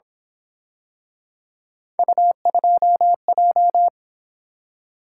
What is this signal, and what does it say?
Morse code, heard as a keyed tone, about 26 wpm, 700 Hz. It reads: U2J
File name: Speed-Racing - Three Letter-Number Combinations - 22wpm